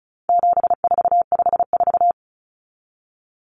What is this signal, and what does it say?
Morse code, heard as a keyed tone, about 34 wpm, 700 Hz. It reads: 7454
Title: Numbers - 4-Digit 35wpm